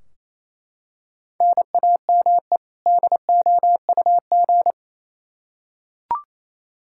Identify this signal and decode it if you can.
Morse code, heard as a keyed tone, about 28 wpm, 700 Hz. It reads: NAME DOUG
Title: Practice QSO - Exchange 1 - Beginning - 28wpm